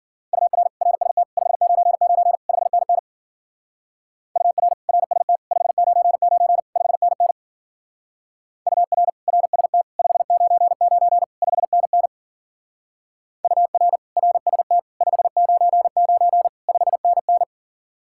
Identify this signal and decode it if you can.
Morse code, heard as a keyed tone, about 49 wpm, 700 Hz. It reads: UR RST 599 5NN UR RST 599 5NN UR RST 599 5NN UR RST 599 5NN
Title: Speed-Racing Common QSO Phrases - 40wpm